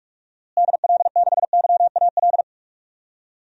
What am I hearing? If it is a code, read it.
Morse code, heard as a keyed tone, about 46 wpm, 700 Hz. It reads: DL6YAL
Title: DXCC-100 Call Signs - 6 Characters - Repeated 3x - 45wpm